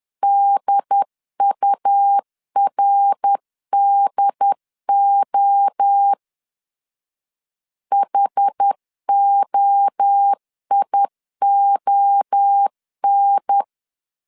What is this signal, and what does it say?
Morse code, heard as a keyed tone, about 10 wpm, 800 Hz. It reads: DURDO HOION
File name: 10wpm